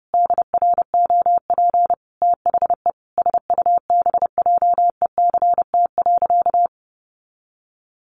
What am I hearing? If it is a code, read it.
Morse code, heard as a keyed tone, about 30 wpm, 700 Hz. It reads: DROP THE SUBJECT.
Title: Sentences from Top 1400 Words 30wpm